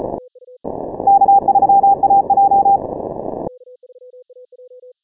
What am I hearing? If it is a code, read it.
Morse code, heard as a keyed tone, about 34 wpm, 790 Hz. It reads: K3AJ
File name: final / W1UJ-168-140-MorseRunner